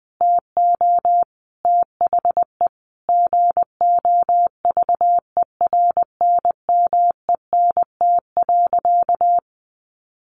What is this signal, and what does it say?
Morse code, heard as a keyed tone, about 20 wpm, 700 Hz. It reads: TO THE GOVERNMENT.
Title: A Sound Of Thunder 20wpm